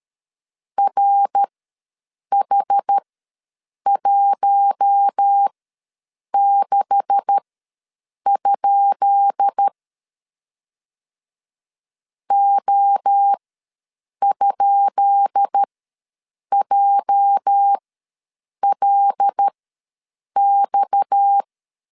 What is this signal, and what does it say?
Morse code, heard as a keyed tone, about 13 wpm, spaced as for 7 wpm, 800 Hz. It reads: RH16? O?JLX
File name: LEZIONI / 10 SIGNS / 10B INT ws 12wpm 3527